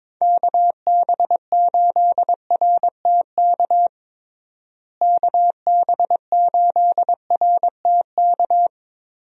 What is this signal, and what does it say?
Morse code, heard as a keyed tone, about 22 wpm, 700 Hz. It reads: KB8RTK KB8RTK
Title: USA Call Signs - 6 Characters - Repeated 2x - 22wpm